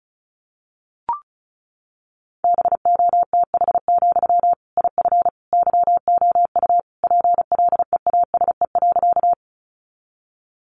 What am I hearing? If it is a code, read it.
Morse code, heard as a keyed tone, about 34 wpm, 700 Hz. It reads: BOTH, IF YOU PLEASE.
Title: Sentences from Top 1500 Words 35wpm